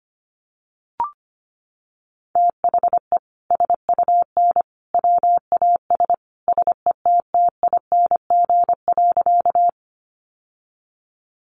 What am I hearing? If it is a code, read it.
Morse code, heard as a keyed tone, about 25 wpm, 700 Hz. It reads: THE SUN WAS SETTING.
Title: Sentences from Top 1300 Words 25wpm